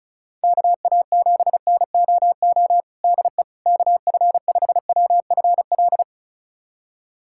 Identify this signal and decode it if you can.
Morse code, heard as a keyed tone, about 34 wpm, 700 Hz. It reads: KA7NOO DE KF5WFL
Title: USA Callsign DE Callsign 35wpm